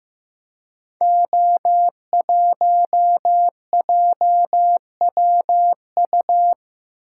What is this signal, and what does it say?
Morse code, heard as a keyed tone, about 15 wpm, 700 Hz. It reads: O1JWU
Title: Code Groups - 5 Letter:Numbers - 15wpm